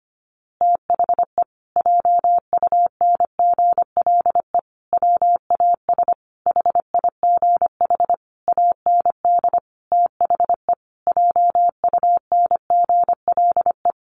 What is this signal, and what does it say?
Morse code, heard as a keyed tone, about 25 wpm, 700 Hz. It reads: THE JUNGLE WAS HIGH AND THE JUNGLE
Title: A Sound Of Thunder 25wpm